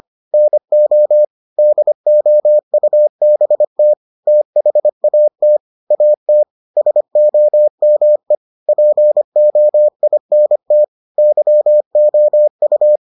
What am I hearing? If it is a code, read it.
Morse code, heard as a keyed tone, about 25 wpm, 600 Hz. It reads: NO DOUBT THAT AT SOME POINT YOU